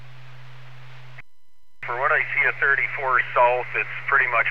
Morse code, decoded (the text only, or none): none